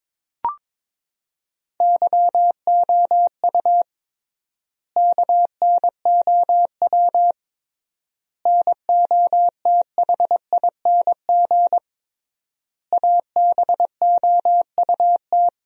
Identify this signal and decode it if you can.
Morse code, heard as a keyed tone, about 22 wpm, 700 Hz. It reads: YOU KNOW NOTHING ABOUT
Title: 3x Word-Spacing - Sentences from Top 500 Words - 22wpm